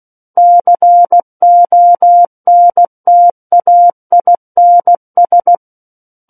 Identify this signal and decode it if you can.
Morse code, heard as a keyed tone, about 16 wpm, 700 Hz. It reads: CONTAINS
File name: Book One / 07-Chapter 7 - 16 WPM